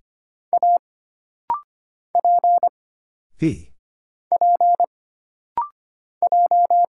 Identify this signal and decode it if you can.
Morse code, heard as a keyed tone, about 25 wpm, 700 Hz. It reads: A P P J
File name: Single Character (Letter:Number:Punctuation:Prosign) - Rapid-Fire - 25wpm